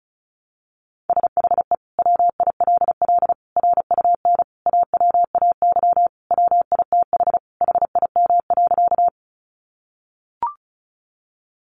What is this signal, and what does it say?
Morse code, heard as a keyed tone, about 34 wpm, 700 Hz. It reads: SHE WILL RUN AWAY WITH HIM.
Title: Sentences from Top 600 Words 35wpm